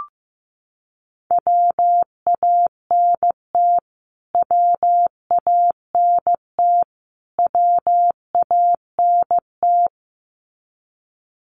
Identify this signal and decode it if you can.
Morse code, heard as a keyed tone, about 15 wpm, 700 Hz. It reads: WANT WANT WANT
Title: Top 500 Words Repeated 15wpm